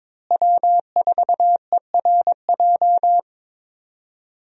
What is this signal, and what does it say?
Morse code, heard as a keyed tone, about 23 wpm, 700 Hz. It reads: W4ERJ
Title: USA Call Signs - 5 Characters - Repeated 3x - 22wpm